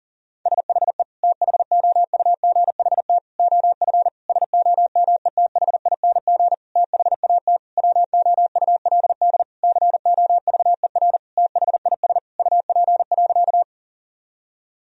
Morse code, heard as a keyed tone, about 40 wpm, 700 Hz. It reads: SHE THOUGHT OF SOMETHING THAT WOULD COVER THIS UP.
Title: Sentences from Top 1000 Words - Review Part 1 - 40wpm